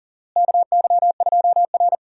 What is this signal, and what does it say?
Morse code, heard as a keyed tone, about 37 wpm, 700 Hz. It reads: KY2R
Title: USA Call Signs - 4 Characters - Repeated 3x - 40wpm